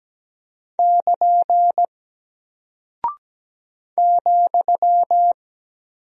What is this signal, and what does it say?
Morse code, heard as a keyed tone, about 17 wpm, 700 Hz. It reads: <KN> ,
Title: Single Character (Letter:Number:Punctuation:Prosign) Alt 17wpm